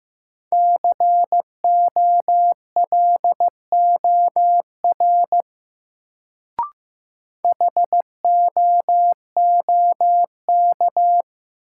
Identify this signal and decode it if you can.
Morse code, heard as a keyed tone, about 15 wpm, 700 Hz. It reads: COLOR HOOK
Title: Sentences from Top 1600 Words - New Word Drill Alt - 15wpm